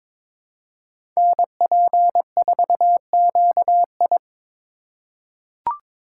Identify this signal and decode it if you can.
Morse code, heard as a keyed tone, about 22 wpm, 700 Hz. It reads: NP4QI
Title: USA Call Signs - 4x Character Spacing - 22wpm